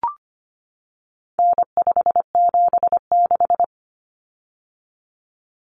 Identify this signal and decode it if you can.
Morse code, heard as a keyed tone, about 25 wpm, 700 Hz. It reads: N576
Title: Common Cut Numbers - 4-Digit 25wpm